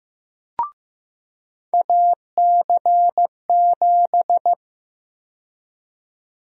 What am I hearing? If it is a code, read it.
Morse code, heard as a keyed tone, about 15 wpm, 700 Hz. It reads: AC7
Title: USA Call Sign Prefixes - 15wpm